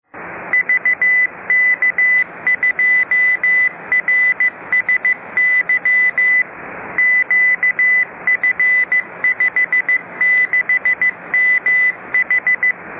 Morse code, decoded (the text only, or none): VK2RSY QF56MH